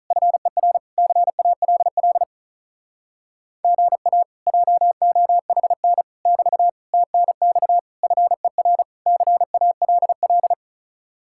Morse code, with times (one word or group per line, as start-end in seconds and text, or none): 0.1-0.8 FER
1.0-2.2 CALL
3.6-4.2 GA
4.5-6.0 JOHN
6.3-6.7 <BT>
6.9-7.8 TNX
8.0-8.8 FER
9.1-10.5 CALL